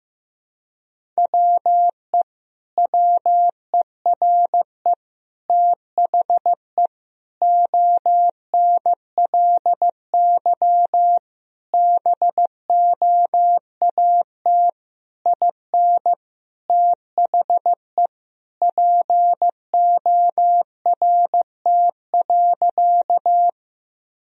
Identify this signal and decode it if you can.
Morse code, heard as a keyed tone, about 15 wpm, 700 Hz. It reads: WE WERE THE ONLY BOAT IN THE PORT.
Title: Sentences from Top 1000 Words - Review Part 1 - 15wpm